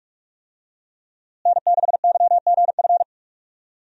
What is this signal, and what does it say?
Morse code, heard as a keyed tone, about 41 wpm, 700 Hz. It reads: N6YGF
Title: USA Call Signs - 5 Characters - Repeated 3x - 45wpm